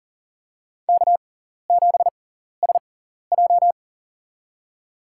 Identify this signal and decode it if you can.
Morse code, heard as a keyed tone, about 40 wpm, spaced as for 14 wpm, 700 Hz. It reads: K7SJ
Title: USA Call Signs - 4 Characters - 6x Character Spacing - 40wpm